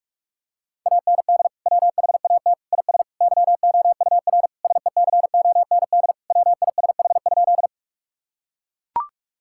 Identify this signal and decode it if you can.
Morse code, heard as a keyed tone, about 41 wpm, 700 Hz. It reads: AND WHAT IS YOUR SECOND WISH?
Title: Sentences from Top 1600 Words 45wpm